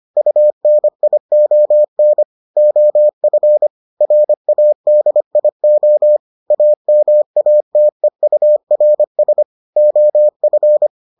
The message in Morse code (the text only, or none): UNION OF RADIO AMATEURS OF